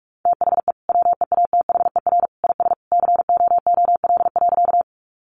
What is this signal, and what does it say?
Morse code, heard as a keyed tone, about 44 wpm, 700 Hz. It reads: THE WEATHER IS COOL.
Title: Sentences from Top 1200 Words 45wpm